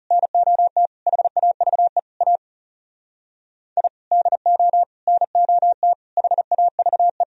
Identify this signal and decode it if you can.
Morse code, heard as a keyed tone, about 37 wpm, 700 Hz. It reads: NOT HAVE A I DO NOT HAVE
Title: Speed-Racing - Sets of 5 Words 30wpm